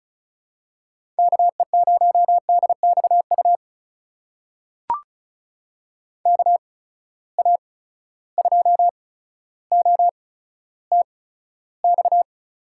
Morse code, heard as a keyed tone, about 36 wpm, spaced as for 15 wpm, 700 Hz. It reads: KE0DXU KA2OTX
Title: USA Call Signs - 6 Characters - 8x Character Spacing - 35wpm